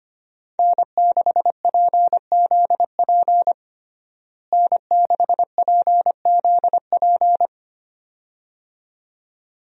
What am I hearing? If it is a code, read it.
Morse code, heard as a keyed tone, about 25 wpm, 700 Hz. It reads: N6PZP N6PZP
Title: USA Call Signs - 5 Characters - Repeated 2x - 25wpm